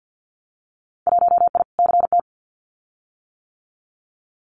Morse code, heard as a keyed tone, about 50 wpm, 700 Hz. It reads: 1I CT
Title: Winter Field Day Exchanges 50wpm